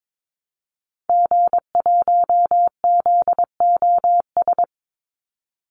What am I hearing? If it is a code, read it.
Morse code, heard as a keyed tone, about 22 wpm, 700 Hz. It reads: G1ZOS
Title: Call Signs - 22wpm